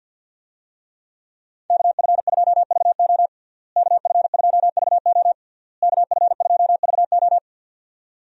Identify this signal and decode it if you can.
Morse code, heard as a keyed tone, about 48 wpm, 700 Hz. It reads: KF2VO KF2VO KF2VO
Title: USA Call Signs - 5 Characters - Repeated 3x - 50wpm